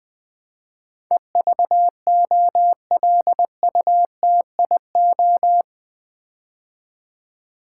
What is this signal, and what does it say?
Morse code, heard as a keyed tone, about 20 wpm, 700 Hz. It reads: EVOLUTIO
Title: Build A Word - 4 Letters At A Time - 12-Letter Words - 20wpm